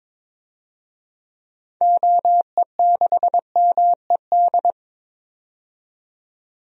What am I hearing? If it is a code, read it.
Morse code, heard as a keyed tone, about 21 wpm, 700 Hz. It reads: OE6MED
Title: DXCC-100 Call Signs - 6 Characters - 22wpm